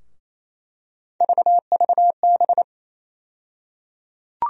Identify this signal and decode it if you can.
Morse code, heard as a keyed tone, about 28 wpm, 700 Hz. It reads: VVB
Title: USA Call Sign Postfixes - 3 Letters 28wpm